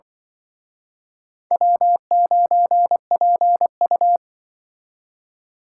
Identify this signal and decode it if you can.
Morse code, heard as a keyed tone, about 25 wpm, 700 Hz. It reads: W9PU
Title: Speed Racing - USA 1x2 Call Signs 20wpm